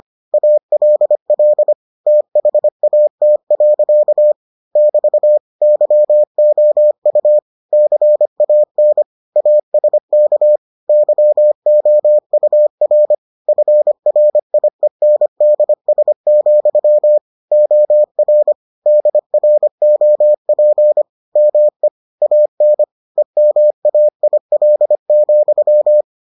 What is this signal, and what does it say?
Morse code, heard as a keyed tone, about 25 wpm, 600 Hz. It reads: ALL THAT. <BT> YOU CAN ASK YOUR FRIENDS, OR DROP ME AN EMAIL,